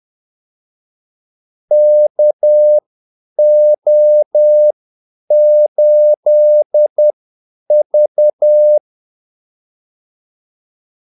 Morse code, heard as a keyed tone, about 10 wpm, spaced as for 8 wpm, 600 Hz. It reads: KO8V
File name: Callsigns-10-8wpm